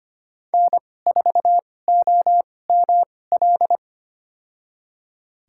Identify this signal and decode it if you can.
Morse code, heard as a keyed tone, about 25 wpm, spaced as for 18 wpm, 700 Hz. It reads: N4OML
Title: USA Call Signs - 5 Characters - 2x Character Spacing - 25wpm